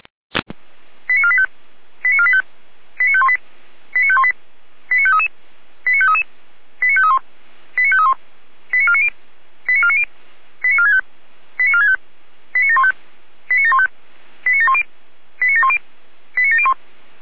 Five-tone selective call (ZVEI), heard as quick runs of five tones, tones of about 70 ms. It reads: 89485 89485 89518 89518 89533 89533 89531 89531 89490 89490 89465 89465 89715 89715 89710 89710 89701